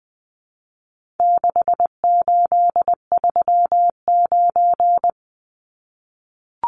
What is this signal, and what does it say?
Morse code, heard as a keyed tone, about 20 wpm, 700 Hz. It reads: 6839
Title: Numbers - 4-Digit 20wpm